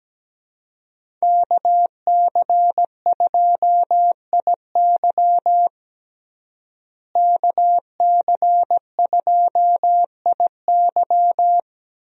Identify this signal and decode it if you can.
Morse code, heard as a keyed tone, about 17 wpm, 700 Hz. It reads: KC2IY KC2IY
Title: USA Call Signs - 5 Characters - Repeated 3x - 17wpm